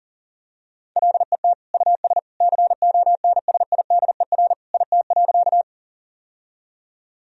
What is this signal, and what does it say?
Morse code, heard as a keyed tone, about 40 wpm, 700 Hz. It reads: LET US CONSIDER IT.
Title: Sentences from Top 1000 Words - Review Part 4 - 40wpm